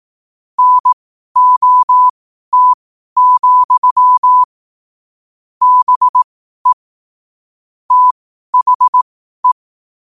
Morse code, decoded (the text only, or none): NOT, BE THE